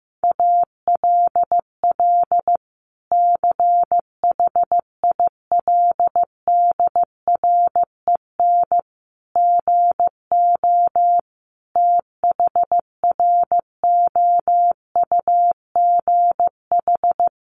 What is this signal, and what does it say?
Morse code, heard as a keyed tone, about 15 wpm, 700 Hz. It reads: ALL CHILDREN GO THROUGH
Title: Sentences from Top 1200 Words 15wpm